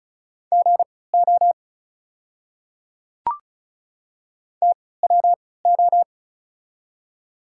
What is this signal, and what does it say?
Morse code, heard as a keyed tone, about 36 wpm, spaced as for 20 wpm, 700 Hz. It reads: GO TWO
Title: 3x Character Spacing - Top 200 Words - 35wpm